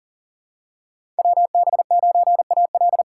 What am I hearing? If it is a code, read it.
Morse code, heard as a keyed tone, about 40 wpm, 700 Hz. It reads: WB9AL
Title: USA Call Signs - 5 Characters - 2x Character Spacing - 40wpm